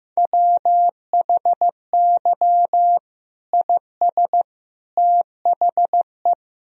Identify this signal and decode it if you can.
Morse code, heard as a keyed tone, about 16 wpm, 700 Hz. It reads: WHY IS THE
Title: Sentences from Top 1000 Words - Review Part 2 - 15wpm